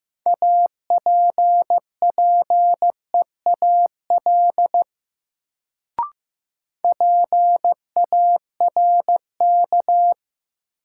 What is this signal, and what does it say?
Morse code, heard as a keyed tone, about 15 wpm, 700 Hz. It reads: APPEAL PARK
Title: Sentences from Top 1300 Words - New Word Drill - 15wpm